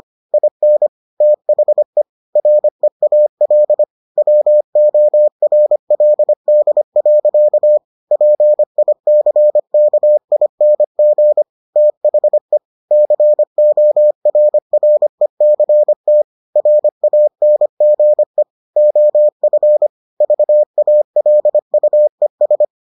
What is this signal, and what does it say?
Morse code, heard as a keyed tone, about 25 wpm, 600 Hz. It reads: IN THE REAL WORLD. PICKING THE CORRECT RANGE OF VALUES